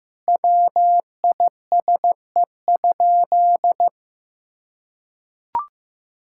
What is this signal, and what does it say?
Morse code, heard as a keyed tone, about 15 wpm, 700 Hz. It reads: WISE?
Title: Sentences from Top 1600 Words 15wpm